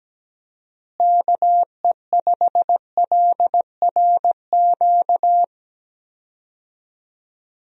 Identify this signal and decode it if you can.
Morse code, heard as a keyed tone, about 17 wpm, 700 Hz. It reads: KE5LRQ
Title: USA Call Signs - 6 Characters - Repeated 2x - 17wpm